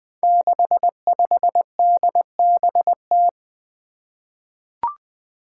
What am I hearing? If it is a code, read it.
Morse code, heard as a keyed tone, about 21 wpm, 700 Hz. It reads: 65DBT
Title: Five Challenging Letter-Number Combinations 20wpm